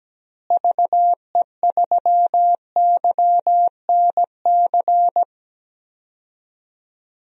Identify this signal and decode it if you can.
Morse code, heard as a keyed tone, about 17 wpm, 700 Hz. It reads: VE3YNC
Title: DXCC-100 Call Signs - 6 Characters - Repeated 2x - 17wpm